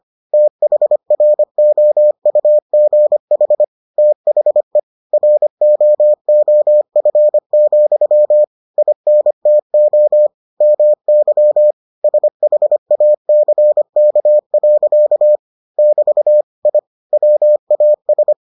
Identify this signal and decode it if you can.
Morse code, heard as a keyed tone, about 25 wpm, 600 Hz. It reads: THROUGH THE ROOF, INTO MY SHACK. <BT> I WAS